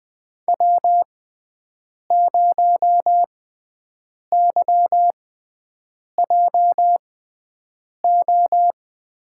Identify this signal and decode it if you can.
Morse code, heard as a keyed tone, about 20 wpm, spaced as for 7 wpm, 700 Hz. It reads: W0YJO
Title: USA Call Signs - 5 Characters - 6x Character Spacing - 20wpm